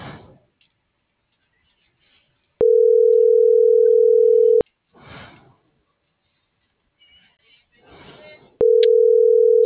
Ringback tone (North America), 440 + 480 Hz.